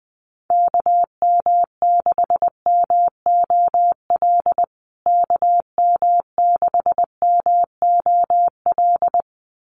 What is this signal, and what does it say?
Morse code, heard as a keyed tone, about 20 wpm, 700 Hz. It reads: KM6MOL KM6MOL
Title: USA Call Signs Repeated 2x 20wpm